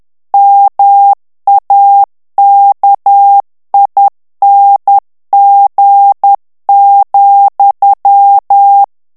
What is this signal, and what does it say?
Morse code, heard as a keyed tone, about 10 wpm, 800 Hz. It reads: MAKING,
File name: ANGLAIS texte clair 3 vitesse 10 mots minute